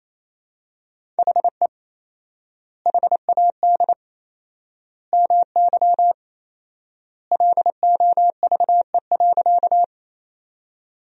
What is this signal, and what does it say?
Morse code, heard as a keyed tone, about 28 wpm, 700 Hz. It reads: HE HAD MY LOVE.
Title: 4x Word-Spacing - Sentences from Top 400 Words - 28wpm